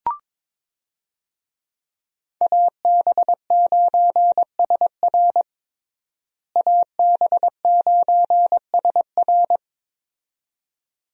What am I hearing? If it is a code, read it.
Morse code, heard as a keyed tone, about 21 wpm, 700 Hz. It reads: AB9SR AB9SR
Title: USA Call Signs - 5 Characters - Repeated 2x - 22wpm